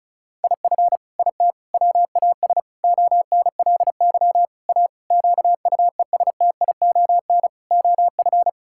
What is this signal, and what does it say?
Morse code, heard as a keyed tone, about 34 wpm, 700 Hz. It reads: IF IT WAS ONLY A QUESTION OF
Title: Sentences from Top 1600 Words 35wpm